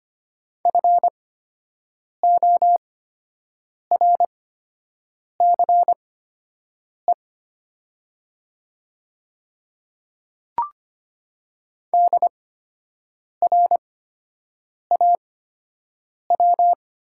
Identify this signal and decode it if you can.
Morse code, heard as a keyed tone, about 25 wpm, spaced as for 7 wpm, 700 Hz. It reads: FORCE DRAW
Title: 8x Character Spacing - Top 500 Words - 25wpm